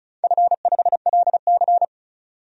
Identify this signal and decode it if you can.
Morse code, heard as a keyed tone, about 36 wpm, 700 Hz. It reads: F5LC
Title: DXCC-100 Call Signs - 4 Characters - 35wpm